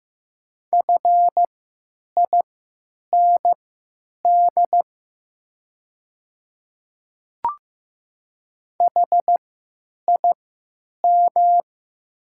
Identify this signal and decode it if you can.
Morse code, heard as a keyed tone, about 15 wpm, spaced as for 8 wpm, 700 Hz. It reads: FIND HIM
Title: 3x Character Spacing - Top 500 Words - 15wpm